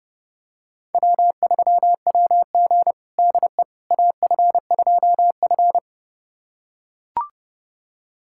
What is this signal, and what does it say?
Morse code, heard as a keyed tone, about 30 wpm, 700 Hz. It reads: W3WG DE AF2F
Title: USA Callsign DE Callsign - 4 Characters - 30wpm